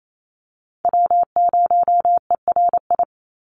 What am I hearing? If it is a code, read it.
Morse code, heard as a keyed tone, about 28 wpm, 700 Hz. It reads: W0ERI
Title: NA SOTA Chaser Callsigns 28wpm